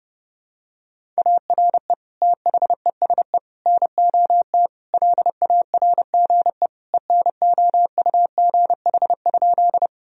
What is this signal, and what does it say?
Morse code, heard as a keyed tone, about 30 wpm, 700 Hz. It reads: ARE THESE NOT LARGE ENOUGH?
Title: Sentences from Top 1000 Words - Review Part 1 - 30wpm